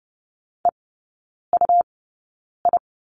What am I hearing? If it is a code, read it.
Morse code, heard as a keyed tone, about 30 wpm, spaced as for 9 wpm, 700 Hz. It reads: EUI
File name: Lesson 17e - Letter u - Three Characters 30wpm